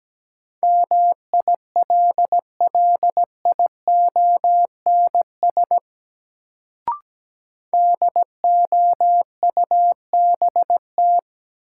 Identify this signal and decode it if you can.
Morse code, heard as a keyed tone, about 17 wpm, 700 Hz. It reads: MILLIONS DOUBT
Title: Sets of 5 Words - New Word Drill - 17wpm